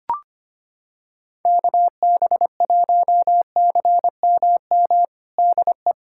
Morse code, heard as a keyed tone, about 25 wpm, 700 Hz. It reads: KB1CMM DE
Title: USA Callsign DE Callsign - 6 Characters - 25wpm